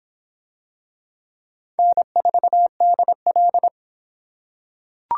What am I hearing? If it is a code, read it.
Morse code, heard as a keyed tone, about 25 wpm, 700 Hz. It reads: N4DL